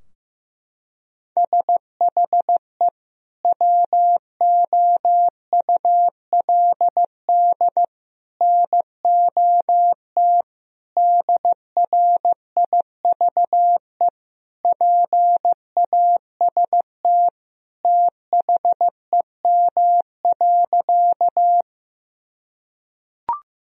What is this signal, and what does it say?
Morse code, heard as a keyed tone, about 15 wpm, 700 Hz. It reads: SHE WOULD NOT DRIVE PAST THEM.
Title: Sentences from Top 1000 Words - Review Part 3 - 15wpm